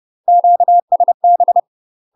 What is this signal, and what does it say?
Morse code, heard as a keyed tone, about 30 wpm, 700 Hz. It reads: QSB
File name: UK MASTER QSOs TRAINING / 30WPM / UK MASTER QSOs 0023